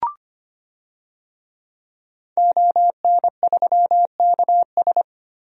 Morse code, heard as a keyed tone, about 25 wpm, 700 Hz. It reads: ON3KS
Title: DXCC-100 Call Signs - 5 Characters - 25wpm